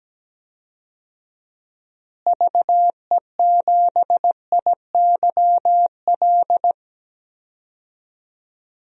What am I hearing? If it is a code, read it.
Morse code, heard as a keyed tone, about 17 wpm, 700 Hz. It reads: VE7IYL
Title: DXCC-100 Call Signs - 6 Characters - Repeated 3x - 17wpm